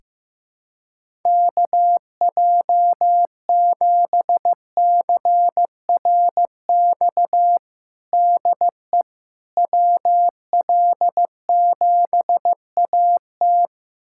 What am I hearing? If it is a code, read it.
Morse code, heard as a keyed tone, about 15 wpm, 700 Hz. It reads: KJ7CRX DE WL7AT
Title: USA Callsign DE Callsign 15wpm